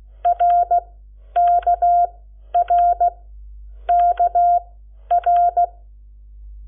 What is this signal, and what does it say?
Morse code, heard as a keyed tone, about 16 wpm, spaced as for 10 wpm, 700 Hz. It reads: RKRKR
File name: Lecciones / CLASE Nº7